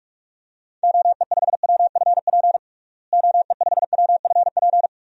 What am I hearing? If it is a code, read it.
Morse code, heard as a keyed tone, about 46 wpm, 700 Hz. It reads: OE5WFP OE5WFP
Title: DXCC-100 Call Signs - 6 Characters - Repeated 2x - 45wpm